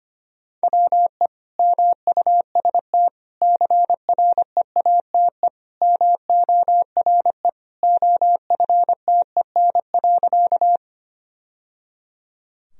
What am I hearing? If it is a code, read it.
Morse code, heard as a keyed tone, about 25 wpm, 700 Hz. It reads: WE MUST CREATE MORE OFTEN.
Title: Sentences from Top 1000 Words - Review Part 3 - 25wpm